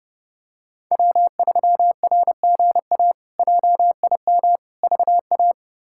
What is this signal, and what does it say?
Morse code, heard as a keyed tone, about 32 wpm, 700 Hz. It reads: W3RGA JIM VA
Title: K1USN SST QSO Practice - 30wpm